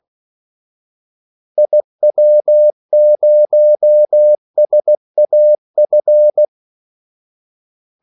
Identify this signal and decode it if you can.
Morse code, heard as a keyed tone, about 16 wpm, 600 Hz. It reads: IW0SAF
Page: Kaldesignaler 16 wpm